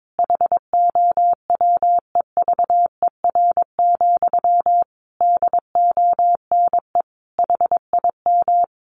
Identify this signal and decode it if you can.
Morse code, heard as a keyed tone, about 22 wpm, 700 Hz. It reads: HOWEVER, DONE HIM
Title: Sentences from Top 1200 Words 22wpm